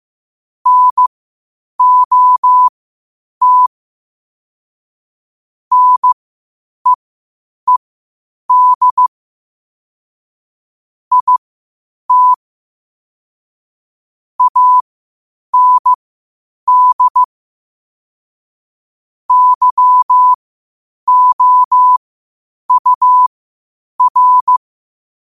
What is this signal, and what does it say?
Morse code, heard as a keyed tone, about 15 wpm, spaced as for 8 wpm, 1000 Hz. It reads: NOT NEED IT AND YOUR
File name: qotdfiles / 8wpm